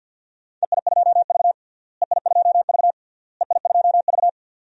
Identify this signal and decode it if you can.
Morse code, heard as a keyed tone, about 51 wpm, 700 Hz. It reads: EI2V EI2V EI2V
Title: DXCC-100 Call Signs - 4 Characters - Repeated 3x - 50wpm